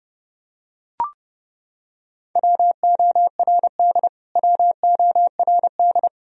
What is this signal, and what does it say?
Morse code, heard as a keyed tone, about 29 wpm, 700 Hz. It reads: WORD WORD
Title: Top 100 Words Repeated 30wpm